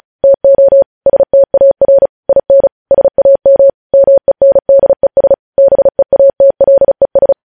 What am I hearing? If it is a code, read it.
Morse code, heard as a keyed tone, about 36 wpm, 550 Hz. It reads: TO STAR IN SAM MENDES BEATLES
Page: news headlines in morse code at 35 wpm